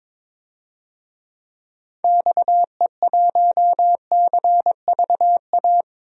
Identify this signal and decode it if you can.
Morse code, heard as a keyed tone, about 22 wpm, 700 Hz. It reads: XE1CVA
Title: DXCC-100 Call Signs - 6 Characters - 22wpm